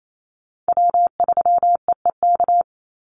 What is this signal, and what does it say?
Morse code, heard as a keyed tone, about 28 wpm, 700 Hz. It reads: W3EEK
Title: Call Signs - 28wpm